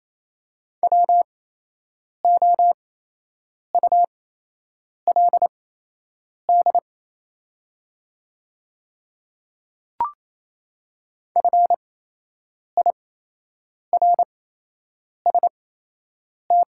Morse code, heard as a keyed tone, about 27 wpm, spaced as for 8 wpm, 700 Hz. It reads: WOULD FIRST
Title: 8x Character Spacing - Top 100 Words - 28wpm